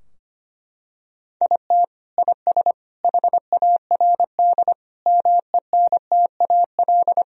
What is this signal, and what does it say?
Morse code, heard as a keyed tone, about 25 wpm, 700 Hz. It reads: IT IS HARD MENTAL
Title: Sentences from Top 1000 Words - Review Part 4 - 25wpm